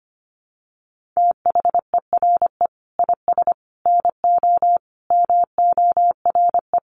Morse code, heard as a keyed tone, about 25 wpm, 700 Hz. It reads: THERE IS NO MORE
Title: Sentences from Top 1400 Words 25wpm